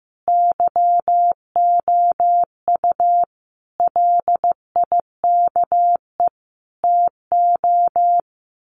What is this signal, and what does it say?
Morse code, heard as a keyed tone, about 15 wpm, 700 Hz. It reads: YOU LIKE TO
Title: Sets of 4 Words 15wpm